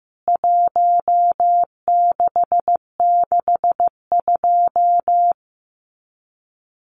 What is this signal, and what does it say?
Morse code, heard as a keyed tone, about 15 wpm, 700 Hz. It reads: 1662
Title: CWT Exchange 15wpm